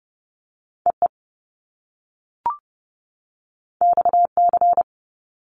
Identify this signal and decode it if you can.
Morse code, heard as a keyed tone, about 30 wpm, 700 Hz. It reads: EE XC
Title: Top Two Letter Combinations in Words 30wpm